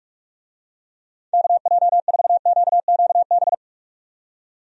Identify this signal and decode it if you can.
Morse code, heard as a keyed tone, about 41 wpm, 700 Hz. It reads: KJ4QQB
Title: USA Call Signs - 6 Characters - 45wpm